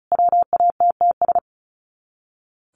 Morse code, heard as a keyed tone, about 34 wpm, 700 Hz. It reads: WATTS
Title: Common QSO Phrases 35wpm